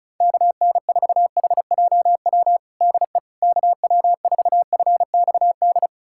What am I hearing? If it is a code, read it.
Morse code, heard as a keyed tone, about 36 wpm, 700 Hz. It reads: KN4HJW DE KW4FXD
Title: USA Callsign DE Callsign - 6 Characters - 35wpm